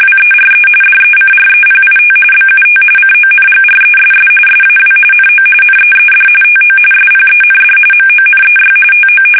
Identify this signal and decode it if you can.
RTTY, about 45.45 baud. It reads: WERE DECREPIT BUT CONGRESS SAW NO NEED TO SPEND MONEY TO
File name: fsk850 / usnavy3